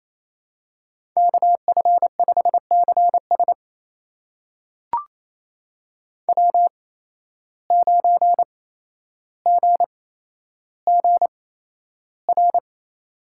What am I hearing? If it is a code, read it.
Morse code, heard as a keyed tone, about 27 wpm, spaced as for 12 wpm, 700 Hz. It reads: KF5CS W9GGR
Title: USA Call Signs - 5 Characters - 8x Character Spacing - 28wpm